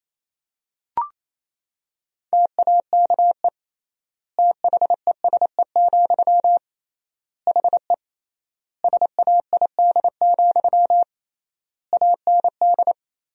Morse code, heard as a keyed tone, about 28 wpm, 700 Hz. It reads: TAKE THESE, HE SAID, AND
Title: 3x Word-Spacing - Sentences from Top 300 Words - 28wpm